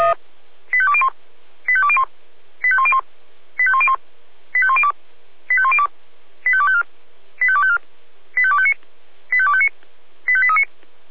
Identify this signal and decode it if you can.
Five-tone selective call (ZVEI), heard as quick runs of five tones, tones of about 70 ms. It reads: 85201 85201 85101 85101 85102 85102 85254 85254 85259 85259 85839